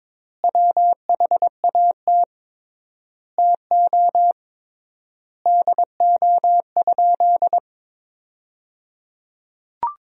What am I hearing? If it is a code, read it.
Morse code, heard as a keyed tone, about 22 wpm, 700 Hz. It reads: WHAT TO DO?
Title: 3x Word-Spacing - Sentences from Top 400 Words - 22wpm